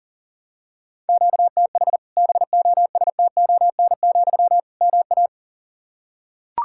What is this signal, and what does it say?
Morse code, heard as a keyed tone, about 40 wpm, 700 Hz. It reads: QTH BOSTON, MA
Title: Practice QSO - Exchange 1 - Beginning - 40wpm